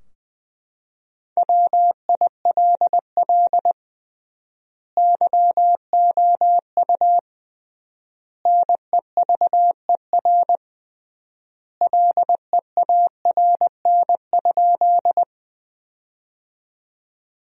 Morse code, heard as a keyed tone, about 20 wpm, 700 Hz. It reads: WILL YOU NEVER LEARN?
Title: 3x Word-Spacing - Sentences from Top 300 Words - 20wpm